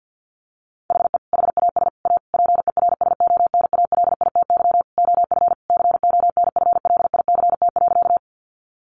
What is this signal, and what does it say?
Morse code, heard as a keyed tone, about 50 wpm, 700 Hz. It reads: HE HAS A PERSONALITY OF CONFLICT.
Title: Sentences from Top 1400 Words 50wpm